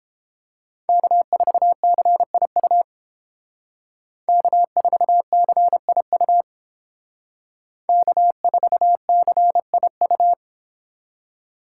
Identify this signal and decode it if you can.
Morse code, heard as a keyed tone, about 30 wpm, 700 Hz. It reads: K4CIU K4CIU K4CIU